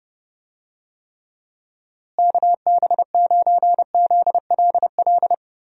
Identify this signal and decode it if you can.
Morse code, heard as a keyed tone, about 29 wpm, 700 Hz. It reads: KB9ZLL
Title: USA Call Signs - 6 Characters - Repeated 3x - 30wpm